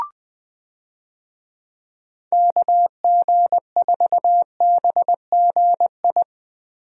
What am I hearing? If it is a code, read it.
Morse code, heard as a keyed tone, about 20 wpm, 700 Hz. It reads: KG4BGI
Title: USA Call Signs - 6 Characters - Repeated 2x - 20wpm